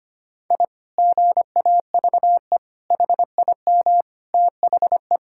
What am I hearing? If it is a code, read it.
Morse code, heard as a keyed tone, about 25 wpm, 700 Hz. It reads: I GAVE HIM THE
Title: Sentences from Top 1000 Words - Review Part 6 - 25wpm